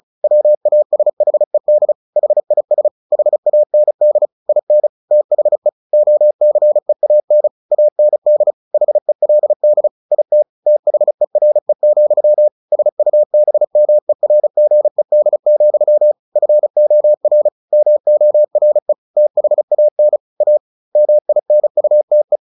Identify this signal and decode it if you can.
Morse code, heard as a keyed tone, about 34 wpm, 600 Hz. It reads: WASHED HIS HAND IN THE OCEAN AND HELD IT THERE, SUBMERGED, FOR MORE THAN A MINUTE
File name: the old man and the sea / the old man and the sea 1 35wpm